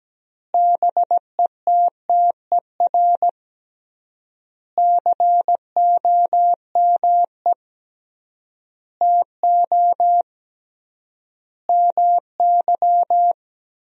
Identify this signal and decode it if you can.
Morse code, heard as a keyed tone, about 17 wpm, 700 Hz. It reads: BETTER COME TO MY
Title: 3x Word-Spacing - Sentences from Top 400 Words - 17wpm